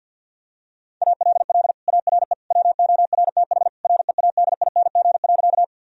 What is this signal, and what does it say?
Morse code, heard as a keyed tone, about 52 wpm, 700 Hz. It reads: ALL ARE WORTH READING.
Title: Sentences from Top 1000 Words - Review Part 3 - 50wpm